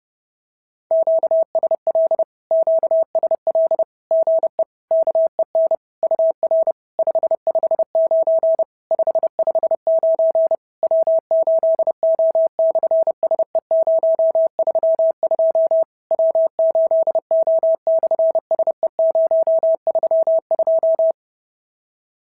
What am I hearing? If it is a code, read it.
Morse code, heard as a keyed tone, about 32 wpm, 650 Hz. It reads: QSL QSL GE KEN UR 559 559 W8O/SE032 W8O/SE032
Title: SOTA QSO Practice 30wpm